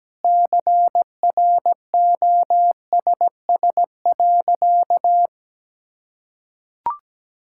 Morse code, heard as a keyed tone, about 17 wpm, 700 Hz. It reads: CROSS.
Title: Sentences from Top 1000 Words - Review Part 6 - 17wpm